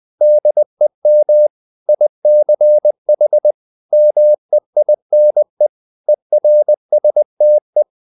Morse code, heard as a keyed tone, about 20 wpm, 600 Hz. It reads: DEM ICH MEINE ERSTE
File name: dk5ke / Humor-100BpM